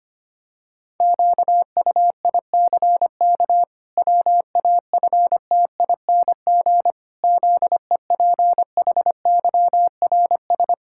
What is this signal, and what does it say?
Morse code, heard as a keyed tone, about 25 wpm, 700 Hz. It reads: QUICK WAFTING ZEPHYRS
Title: Pangram Challenge v2 - 25wpm